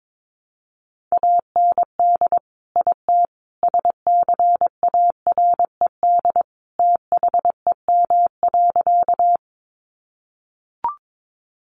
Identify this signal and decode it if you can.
Morse code, heard as a keyed tone, about 22 wpm, 700 Hz. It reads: AND IT SCARED THEM.
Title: Sentences from Top 700 Words 22wpm